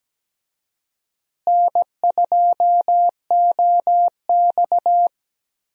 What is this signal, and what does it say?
Morse code, heard as a keyed tone, about 17 wpm, 700 Hz. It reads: N2OX
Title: USA Call Signs - 4 Characters - Repeated 3x - 17wpm